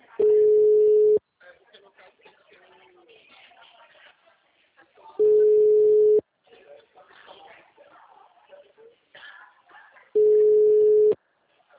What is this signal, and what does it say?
Ringback tone (Europe, 425 Hz), with speech.
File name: out- -2011- - - .4818